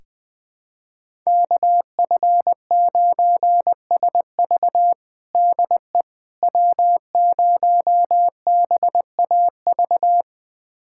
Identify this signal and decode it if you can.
Morse code, heard as a keyed tone, about 21 wpm, 700 Hz. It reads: KF9SV DE W0BAV
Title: USA Callsign DE Callsign - 5 Characters - 20wpm